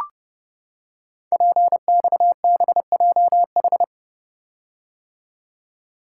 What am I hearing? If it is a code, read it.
Morse code, heard as a keyed tone, about 30 wpm, 700 Hz. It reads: PXBJH
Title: Code Groups - 5 Letters - 30wpm